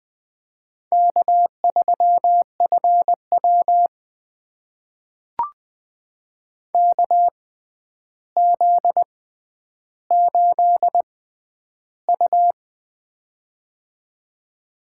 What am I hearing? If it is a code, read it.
Morse code, heard as a keyed tone, about 20 wpm, spaced as for 10 wpm, 700 Hz. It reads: K3FW KZ8U
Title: USA Call Signs - 4 Characters - 6x Character Spacing - 20wpm